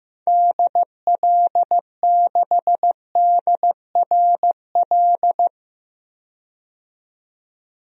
Morse code, heard as a keyed tone, about 15 wpm, 700 Hz. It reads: DL6DRL
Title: DXCC-100 Call Signs - 6 Characters - 15wpm